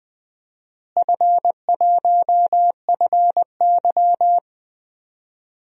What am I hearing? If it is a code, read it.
Morse code, heard as a keyed tone, about 20 wpm, 700 Hz. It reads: F1FY
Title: DXCC-100 Call Signs - 4 Characters - Repeated 3x - 20wpm